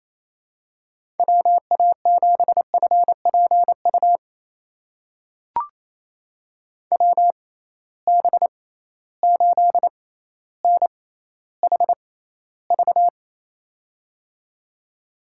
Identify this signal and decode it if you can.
Morse code, heard as a keyed tone, about 28 wpm, spaced as for 14 wpm, 700 Hz. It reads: WA7FPU WB8NHV
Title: USA Call Signs - 6 Characters - 6x Character Spacing - 28wpm